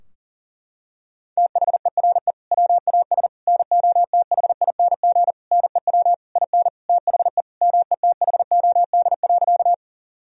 Morse code, heard as a keyed tone, about 37 wpm, 700 Hz. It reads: THERE WAS NOTHING NEW IN THE METHOD.
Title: Sentences from Top 1000 Words - Review Part 3 - 40wpm